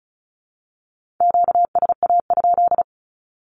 Morse code, heard as a keyed tone, about 36 wpm, 700 Hz. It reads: QSA?
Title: Q-Codes with Spoken Meaning - 35wpm